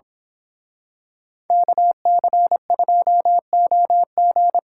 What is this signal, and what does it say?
Morse code, heard as a keyed tone, about 25 wpm, 700 Hz. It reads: KC2OG
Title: Speed Racing - USA 2x2 Call Signs 22wpm